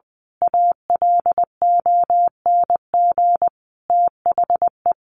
Morse code, heard as a keyed tone, about 20 wpm, 700 Hz. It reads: ALONG THE
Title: A Sound Of Thunder 20wpm